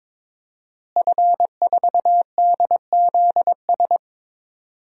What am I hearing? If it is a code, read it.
Morse code, heard as a keyed tone, about 22 wpm, 700 Hz. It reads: F4DZS
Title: DXCC-100 Call Signs - 5 Characters - Repeated 3x - 22wpm